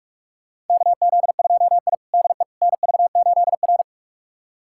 Morse code, heard as a keyed tone, about 42 wpm, 700 Hz. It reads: KZ2I DE NV8R
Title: USA Callsign DE Callsign - 4 Characters - 45wpm